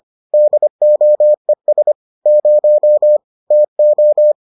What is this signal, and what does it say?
Morse code, heard as a keyed tone, about 25 wpm, 600 Hz. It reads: DOES 0 TO